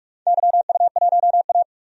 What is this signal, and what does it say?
Morse code, heard as a keyed tone, about 46 wpm, 700 Hz. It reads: YU1A
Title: DXCC-100 Call Signs - 4 Characters - 45wpm